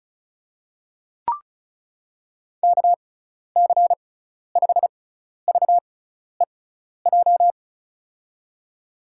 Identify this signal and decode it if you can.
Morse code, heard as a keyed tone, about 36 wpm, spaced as for 12 wpm, 690 Hz. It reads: KC5VEJ
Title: USA Call Signs - 6 Characters - 6x Character Spacing - 35wpm